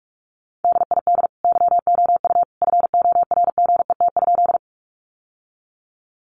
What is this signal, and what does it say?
Morse code, heard as a keyed tone, about 44 wpm, 700 Hz. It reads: DID YOU FORGET?
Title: Sentences from Top 1200 Words 45wpm